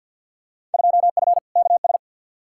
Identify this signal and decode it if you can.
Morse code, heard as a keyed tone, about 53 wpm, 700 Hz. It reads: 2F KS
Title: Field Day Exchanges - 50wpm